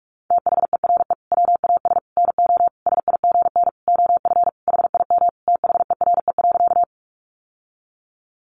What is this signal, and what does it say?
Morse code, heard as a keyed tone, about 44 wpm, 700 Hz. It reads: THERE WAS NO SIGN OF HIM THERE.
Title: Sentences from Top 1000 Words 45wpm